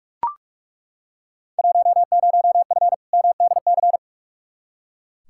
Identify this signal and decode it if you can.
Morse code, heard as a keyed tone, about 41 wpm, 700 Hz. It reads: 10F MDC
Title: Field Day Exchanges Encore - 45wpm